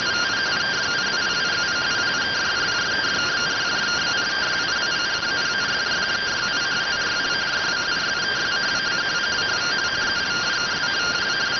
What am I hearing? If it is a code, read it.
Two-tone teleprinter signal, about 170 Hz shift.